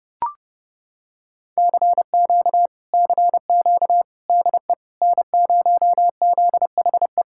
Morse code, heard as a keyed tone, about 29 wpm, 700 Hz. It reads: CQ CQ DE N0ZHE